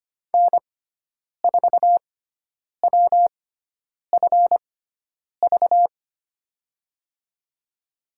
Morse code, heard as a keyed tone, about 25 wpm, spaced as for 9 wpm, 690 Hz. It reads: N4WFV
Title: USA Call Signs - 5 Characters - 6x Character Spacing - 25wpm